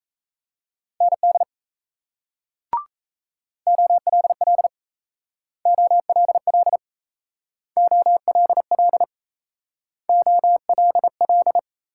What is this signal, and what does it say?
Morse code, heard as a keyed tone, about 37 wpm, 700 Hz. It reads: ND OLL OLL OLL OLL